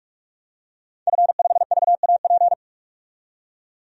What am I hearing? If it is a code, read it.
Morse code, heard as a keyed tone, about 44 wpm, 690 Hz. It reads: F5VAP
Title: DXCC-100 Call Signs - 5 Characters - 45wpm